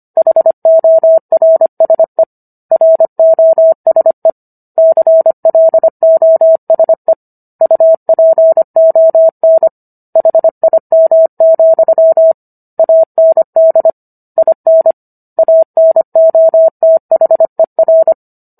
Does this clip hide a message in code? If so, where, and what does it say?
Yes: Morse code from 0.2 to 18.1 s, heard as a keyed tone, about 25 wpm, 650 Hz. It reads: HORSE ROSE CLOSE UPON HIM, AND IN ANOTHER